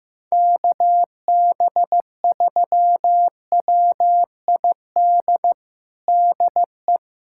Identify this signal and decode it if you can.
Morse code, heard as a keyed tone, about 15 wpm, 700 Hz. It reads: KB3WID DE
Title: USA Callsign DE Callsign 15wpm